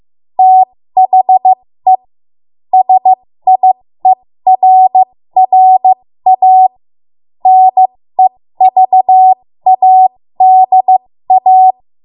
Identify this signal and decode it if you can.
Morse code, heard as a keyed tone, about 15 wpm, spaced as for 13 wpm, 750 Hz. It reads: THE SIERRA NEVADA
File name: Morse / 13WPM